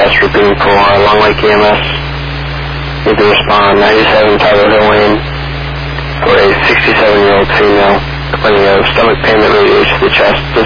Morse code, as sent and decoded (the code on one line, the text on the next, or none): none